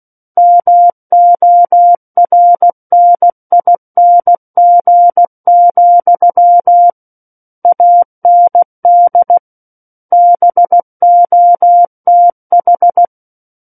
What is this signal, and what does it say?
Morse code, heard as a keyed tone, about 16 wpm, 700 Hz. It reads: MORNING, AND BOTH
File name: Book One / 07-Chapter 7 - 16 WPM